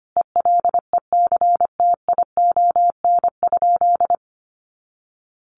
Morse code, heard as a keyed tone, about 25 wpm, 700 Hz. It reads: ELECTION?
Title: Sentences from Top 1500 Words 25wpm